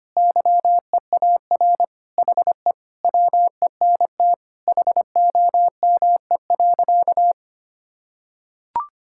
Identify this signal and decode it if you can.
Morse code, heard as a keyed tone, about 25 wpm, 700 Hz. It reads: YEAR HE WENT HOME.
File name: practice / 25wpm / Sentences with 4-Word Phrases - 25wpm